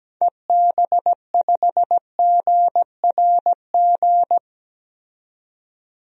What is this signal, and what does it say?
Morse code, heard as a keyed tone, about 17 wpm, 700 Hz. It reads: EB5GRG
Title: DXCC-100 Call Signs - 6 Characters - 17wpm